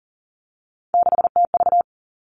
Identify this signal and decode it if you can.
Morse code, heard as a keyed tone, about 38 wpm, 700 Hz. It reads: 6TV
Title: Three Challenging Letter-Number Combinations 40wpm